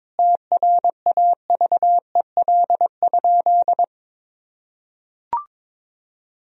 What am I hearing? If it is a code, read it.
Morse code, heard as a keyed tone, about 22 wpm, 700 Hz. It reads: TRAVEL?
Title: Sentences from Top 1000 Words - Review Part 2 - 22wpm